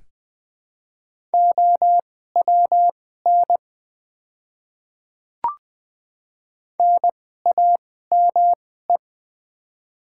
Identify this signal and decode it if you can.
Morse code, heard as a keyed tone, about 20 wpm, spaced as for 14 wpm, 700 Hz. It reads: OWN NAME
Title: 2x Character Spacing - Top 500 Words - 20wpm